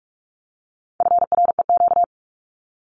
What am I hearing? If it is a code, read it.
Morse code, heard as a keyed tone, about 44 wpm, 700 Hz. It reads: FREQ
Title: Abbreviations 45wpm